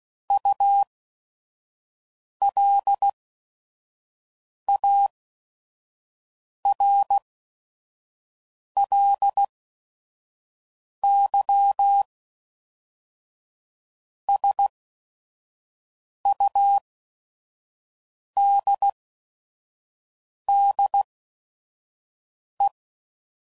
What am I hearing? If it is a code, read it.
Morse code, heard as a keyed tone, about 16 wpm, 800 Hz. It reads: U L A R L Y S U D D E